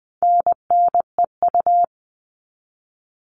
Morse code, heard as a keyed tone, about 20 wpm, 700 Hz. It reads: NNEU
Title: All Cut Numbers - 4-Digit 20wpm